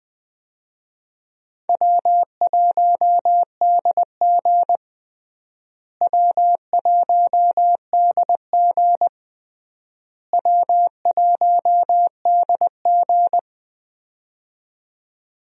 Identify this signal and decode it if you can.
Morse code, heard as a keyed tone, about 20 wpm, 700 Hz. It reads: W1DG W1DG W1DG
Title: USA Call Signs - 4 Characters - Repeated 3x - 20wpm